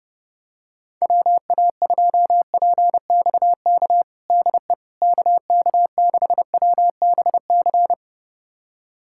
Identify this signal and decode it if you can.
Morse code, heard as a keyed tone, about 30 wpm, 700 Hz. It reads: WA2PXK DE KK6WBC
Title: USA Callsign DE Callsign - 6 Characters - 30wpm